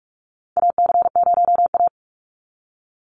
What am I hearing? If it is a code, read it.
Morse code, heard as a keyed tone, about 46 wpm, 700 Hz. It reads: AC0A
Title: NA SOTA Chaser Callsigns 45wpm